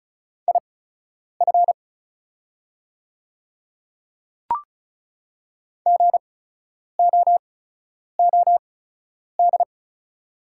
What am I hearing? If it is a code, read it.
Morse code, heard as a keyed tone, about 36 wpm, spaced as for 10 wpm, 690 Hz. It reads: IF GOOD
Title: 8x Character Spacing - Top 200 Words - 35wpm